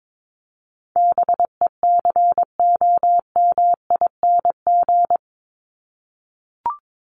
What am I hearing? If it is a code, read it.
Morse code, heard as a keyed tone, about 22 wpm, 700 Hz. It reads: BECOMING
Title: ING Suffix - Top 100 Words with Base Words 22wpm